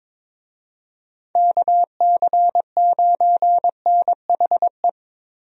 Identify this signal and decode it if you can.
Morse code, heard as a keyed tone, about 21 wpm, 700 Hz. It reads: KC9NHE
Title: USA Call Signs - 6 Characters - Repeated 3x - 22wpm